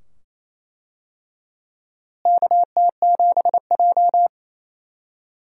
Morse code, heard as a keyed tone, about 28 wpm, 700 Hz. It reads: KT7J
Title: USA Call Signs - 4 Characters - Repeated 2x - 28wpm